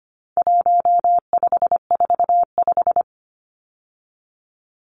Morse code, heard as a keyed tone, about 25 wpm, 700 Hz. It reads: 1545
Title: Numbers - 4-Digit 25wpm